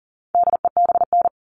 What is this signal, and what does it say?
Morse code, heard as a keyed tone, about 40 wpm, 700 Hz. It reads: DEBN
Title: All Cut Numbers - 4-Digit 40wpm